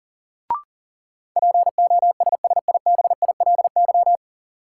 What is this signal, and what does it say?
Morse code, heard as a keyed tone, about 40 wpm, 700 Hz. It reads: POSSIBILY
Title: Sentences from Top 1300 Words - New Word Drill Alt - 40wpm